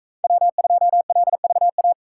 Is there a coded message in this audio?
Yes: Morse code from 0.2 to 1.9 s, heard as a keyed tone, about 40 wpm, 700 Hz. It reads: W2LVA